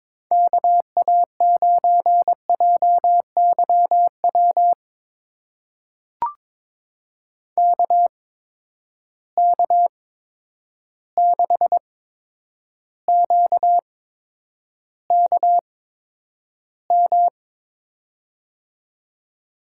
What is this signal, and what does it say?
Morse code, heard as a keyed tone, about 22 wpm, spaced as for 9 wpm, 700 Hz. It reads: KA9JYW KK6QKM
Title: USA Call Signs - 8x Character Spacing - 22wpm